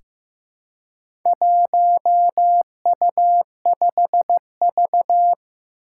Morse code, heard as a keyed tone, about 15 wpm, 700 Hz. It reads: 1U5V
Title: Four Challenging Letter-Number Combinations 15wpm